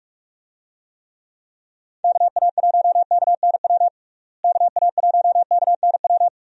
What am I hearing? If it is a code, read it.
Morse code, heard as a keyed tone, about 46 wpm, 700 Hz. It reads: KA1KNW KA1KNW
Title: USA Call Signs - 6 Characters - Repeated 2x - 45wpm